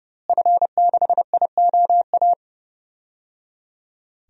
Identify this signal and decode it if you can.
Morse code, heard as a keyed tone, about 30 wpm, 700 Hz. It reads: F6IOA
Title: DXCC-100 Call Signs - 5 Characters - Repeated 3x - 30wpm